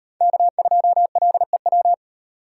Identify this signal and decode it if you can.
Morse code, heard as a keyed tone, about 38 wpm, 700 Hz. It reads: K2LEW